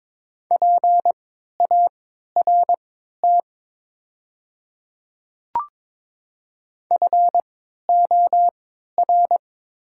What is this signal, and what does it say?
Morse code, heard as a keyed tone, about 23 wpm, spaced as for 13 wpm, 690 Hz. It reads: PART FOR
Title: 3x Character Spacing - Top 500 Words - 22wpm